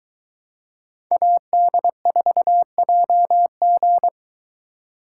Morse code, heard as a keyed tone, about 22 wpm, 700 Hz. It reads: AD4JG